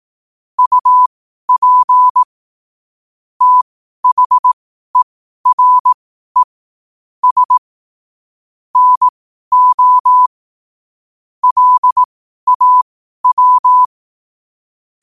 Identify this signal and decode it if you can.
Morse code, heard as a keyed tone, about 18 wpm, spaced as for 12 wpm, 1000 Hz. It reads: UP THERE S NO LAW
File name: qotdfiles / 12wpm